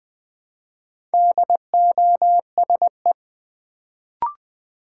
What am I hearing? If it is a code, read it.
Morse code, heard as a keyed tone, about 20 wpm, 700 Hz. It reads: DOSE
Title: Build A Word - 1 Letter At A Time - 4-Letter Words - 20wpm